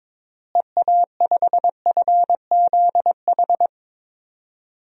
Morse code, heard as a keyed tone, about 22 wpm, 700 Hz. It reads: EA5FZH
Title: DXCC-100 Call Signs - 6 Characters - Repeated 2x - 22wpm